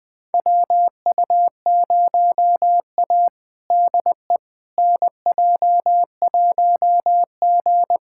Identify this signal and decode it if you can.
Morse code, heard as a keyed tone, about 20 wpm, 700 Hz. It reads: WU0A DE NJ1G
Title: USA Callsign DE Callsign - 4 Characters - 20wpm